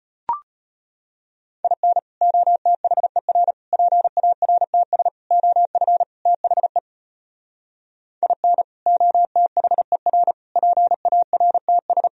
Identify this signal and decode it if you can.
Morse code, heard as a keyed tone, about 33 wpm, 700 Hz. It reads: IN OTHER PARTS OF THE IN OTHER PARTS